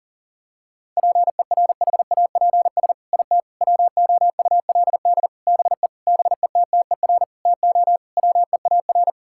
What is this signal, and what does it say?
Morse code, heard as a keyed tone, about 40 wpm, 700 Hz. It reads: PERHAPS IT WOULD BE BETTER TO WEAR
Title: Sentences from Top 1600 Words 40wpm